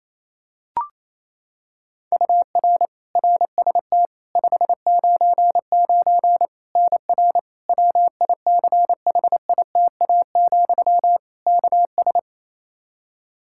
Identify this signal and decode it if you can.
Morse code, heard as a keyed tone, about 28 wpm, 700 Hz. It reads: UR RST 599 NR WICHITA, KS
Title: K1USN SST QSO Practice 28wpm